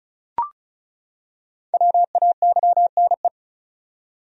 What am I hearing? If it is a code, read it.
Morse code, heard as a keyed tone, about 34 wpm, 700 Hz. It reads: WAYNE
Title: USA Common Names - 35wpm